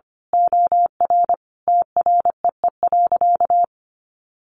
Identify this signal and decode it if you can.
Morse code, heard as a keyed tone, about 25 wpm, 700 Hz. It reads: OR TREE.
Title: A Sound Of Thunder 25wpm